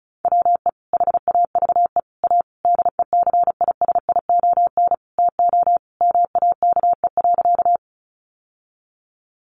Morse code, heard as a keyed tone, about 34 wpm, 700 Hz. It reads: WE HAVE A DECISION TO MAKE.
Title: Sentences from Top 1000 Words 35wpm